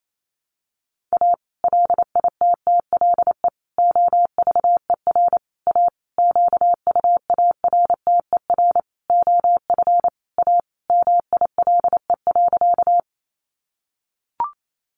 Morse code, heard as a keyed tone, about 28 wpm, 700 Hz. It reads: A LITTLE OVER A QUARTER OF A MILE.
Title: Sentences from Top 1300 Words 28wpm